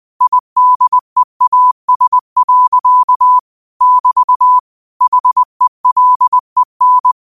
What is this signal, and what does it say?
Morse code, heard as a keyed tone, about 20 wpm, 1000 Hz. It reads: IDEAS. <BT> HELEN